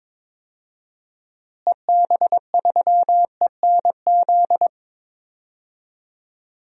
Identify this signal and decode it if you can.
Morse code, heard as a keyed tone, about 22 wpm, 700 Hz. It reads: EB3ENZ
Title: DXCC-100 Call Signs - 6 Characters - Repeated 2x - 22wpm